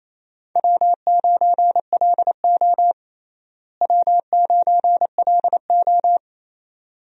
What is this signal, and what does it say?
Morse code, heard as a keyed tone, about 27 wpm, 700 Hz. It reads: W9LO W9LO
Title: USA Call Signs - 4 Characters - Repeated 2x - 28wpm